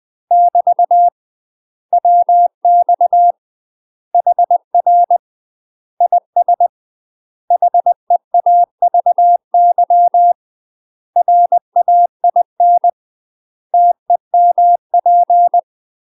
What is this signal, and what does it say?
Morse code, heard as a keyed tone, about 20 wpm, 700 Hz. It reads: <BT> WX HR IS HEAVY RAIN TEMP
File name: UK MASTER QSOs TRAINING / 20WPM / UK MASTER QSOs 0065